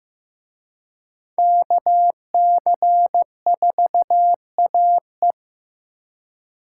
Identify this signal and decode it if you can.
Morse code, heard as a keyed tone, about 18 wpm, 700 Hz. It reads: KC4AE
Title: Speed Racing - USA Call Signs 15wpm